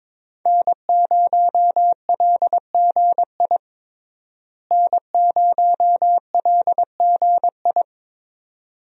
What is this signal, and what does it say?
Morse code, heard as a keyed tone, about 22 wpm, 700 Hz. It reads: N0LGI N0LGI
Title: USA Call Signs - 5 Characters - Repeated 2x - 22wpm